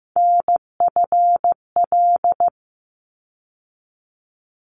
Morse code, heard as a keyed tone, about 15 wpm, 700 Hz. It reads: NFL
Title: ARRL Sections 15wpm